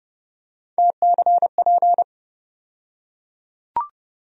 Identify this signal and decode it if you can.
Morse code, heard as a keyed tone, about 30 wpm, 700 Hz. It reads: TCP
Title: USA Call Sign Postfixes - 3 Letters 30wpm